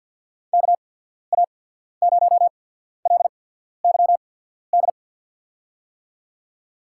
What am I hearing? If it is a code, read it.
Morse code, heard as a keyed tone, about 54 wpm, spaced as for 14 wpm, 690 Hz. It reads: KA0LYD
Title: USA Call Signs - 6 Characters - 8x Character Spacing - 50wpm